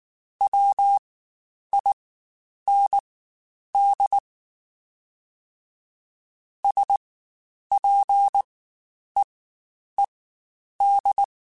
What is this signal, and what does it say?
Morse code, heard as a keyed tone, about 19 wpm, spaced as for 9 wpm, 800 Hz. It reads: WIND SPEED
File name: Spaced Characters / Spaced Characters-mp3s / Weather5 12 sp @18